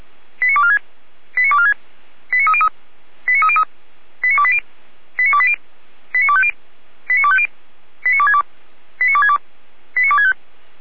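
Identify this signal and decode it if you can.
Five-tone selective call (ZVEI), heard as quick runs of five tones, tones of about 70 ms. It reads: 89246 89246 89303 89303 89280 89280 89250 89250 89262 89262 89265
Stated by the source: ZVEI five-tone fire brigade alerting call received on an SDR.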